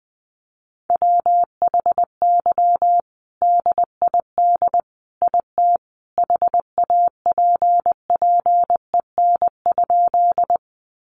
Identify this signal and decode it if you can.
Morse code, heard as a keyed tone, about 20 wpm, 700 Hz. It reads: WHY DID IT HAPPEN?
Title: Sentences from Top 400 Words 20wpm